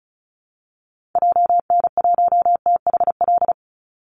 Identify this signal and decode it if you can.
Morse code, heard as a keyed tone, about 34 wpm, 700 Hz. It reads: JN1THL
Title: Call Signs - 35wpm